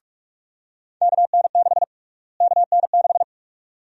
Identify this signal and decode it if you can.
Morse code, heard as a keyed tone, about 43 wpm, 700 Hz. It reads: KN6 KN6
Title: USA Call Sign Prefixes - Repeated 2x - 45wpm